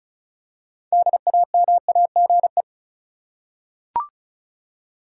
Morse code, heard as a keyed tone, about 36 wpm, 700 Hz. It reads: DAMAGE
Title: Six Letter Words - 35wpm